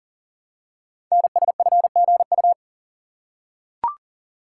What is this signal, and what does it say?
Morse code, heard as a keyed tone, about 40 wpm, 700 Hz. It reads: NSFGU
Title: Code Groups - 5 Letters - 40wpm